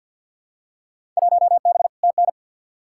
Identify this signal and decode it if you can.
Morse code, heard as a keyed tone, about 48 wpm, 700 Hz. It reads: 1B TN
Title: Field Day Exchanges - 50wpm